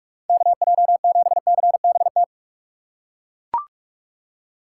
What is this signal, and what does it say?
Morse code, heard as a keyed tone, about 42 wpm, 700 Hz. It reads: KJ7CBT
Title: USA Call Signs - 6 Characters - 3x Character Spacing - 45wpm